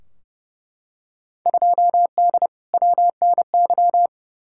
Morse code, heard as a keyed tone, about 29 wpm, 700 Hz. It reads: 2D WNY
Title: Field Day Exchanges - 30wpm